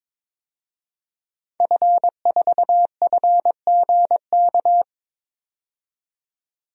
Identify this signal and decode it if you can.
Morse code, heard as a keyed tone, about 21 wpm, 700 Hz. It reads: F4FGK
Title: DXCC-100 Call Signs - 5 Characters - 22wpm